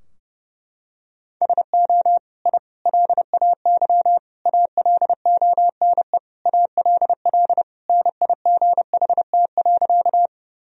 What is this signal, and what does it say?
Morse code, heard as a keyed tone, about 30 wpm, 700 Hz. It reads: SO I LAY ALONE ALL NIGHT.
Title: Sentences from Top 1000 Words - Review Part 2 - 30wpm